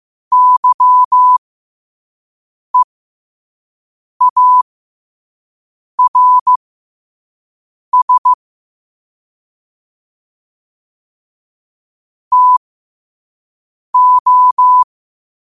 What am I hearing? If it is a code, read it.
Morse code, heard as a keyed tone, about 15 wpm, spaced as for 5 wpm, 1000 Hz. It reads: YEARS TO
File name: qotdfiles / 5wpm